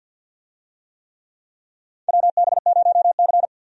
Morse code, heard as a keyed tone, about 50 wpm, 700 Hz. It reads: WB0C
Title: USA Call Signs - 4 Characters - 50wpm